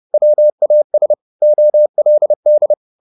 Morse code, heard as a keyed tone, about 30 wpm, 600 Hz. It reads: WAS OLD
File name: the old man and the sea / the old man and the sea 1 30wpm